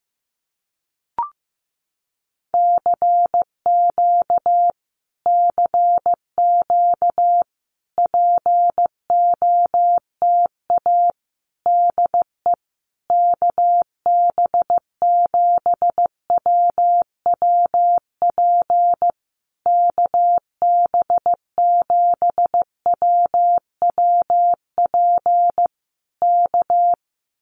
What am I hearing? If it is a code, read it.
Morse code, heard as a keyed tone, about 14 wpm, 700 Hz. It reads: CQ CQ POTA DE KB7WWP KB7WWP K
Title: Common QSO Phrases Encore 15wpm